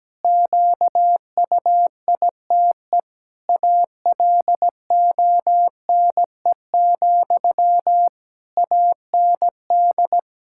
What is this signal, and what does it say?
Morse code, heard as a keyed tone, about 17 wpm, 700 Hz. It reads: QUITE ALONE, AND
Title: Sentences from Top 1600 Words 17wpm